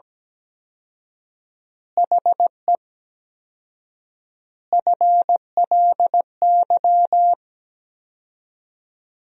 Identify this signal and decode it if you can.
Morse code, heard as a keyed tone, about 17 wpm, 700 Hz. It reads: HE FLY
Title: 4x Word-Spacing - Sentences from Top 500 Words - 17wpm